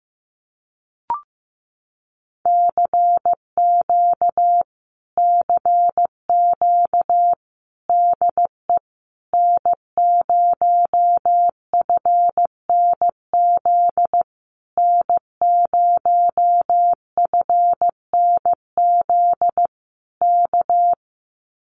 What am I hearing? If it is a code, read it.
Morse code, heard as a keyed tone, about 15 wpm, 700 Hz. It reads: CQ CQ DE N0FNZ N0FNZ K
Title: Common QSO Phrases Encore 15wpm